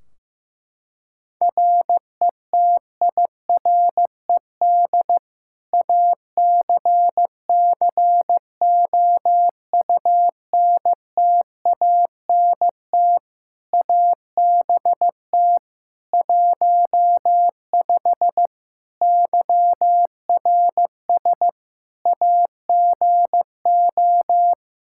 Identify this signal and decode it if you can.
Morse code, heard as a keyed tone, about 15 wpm, 700 Hz. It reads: RETIRED ACCOUNTANT ABT 15 YRS AGO
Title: Practice QSO - Exchange 3 - More Info - 15wpm